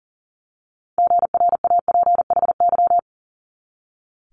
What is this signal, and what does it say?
Morse code, heard as a keyed tone, about 38 wpm, 700 Hz. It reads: GRAPHY
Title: Multi-letter Phonemes 40wpm